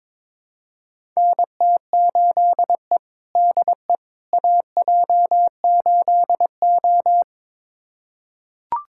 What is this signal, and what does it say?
Morse code, heard as a keyed tone, about 22 wpm, 700 Hz. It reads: NT8E DE AJ8O
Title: USA Callsign DE Callsign - 4 Characters - 22wpm